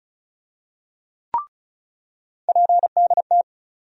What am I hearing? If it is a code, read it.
Morse code, heard as a keyed tone, about 36 wpm, 700 Hz. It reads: PDT
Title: USA Call Sign Postfixes - 3 Letters 35wpm